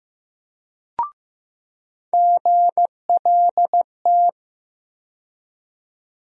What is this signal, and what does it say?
Morse code, heard as a keyed tone, about 15 wpm, 700 Hz. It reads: GLT
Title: USA Call Sign Postfixes - 3 Letters 15wpm